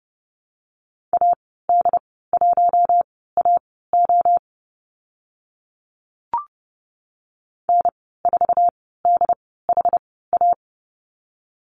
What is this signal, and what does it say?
Morse code, heard as a keyed tone, about 30 wpm, spaced as for 17 wpm, 700 Hz. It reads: AD1AO N4DHA
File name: Lesson 15g - Number 5 - Call Signs 30wpm